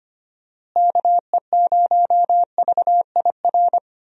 Morse code, heard as a keyed tone, about 25 wpm, 700 Hz. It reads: KE0VIR
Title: USA Call Signs - 6 Characters - Repeated 3x - 25wpm